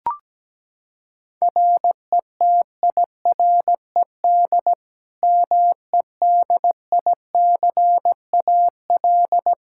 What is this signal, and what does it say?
Morse code, heard as a keyed tone, about 17 wpm, 700 Hz. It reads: RETIRED MEDICAL
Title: K1USN SST QSO Practice 17wpm